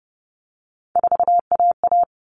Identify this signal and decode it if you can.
Morse code, heard as a keyed tone, about 30 wpm, 700 Hz. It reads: 4AA
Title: All Cut Numbers - 3-Digit 30wpm